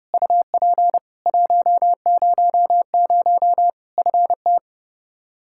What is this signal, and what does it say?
Morse code, heard as a keyed tone, about 30 wpm, 700 Hz. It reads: UP 100 FT
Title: Practice QSO - Exchange 2 - Info - 30wpm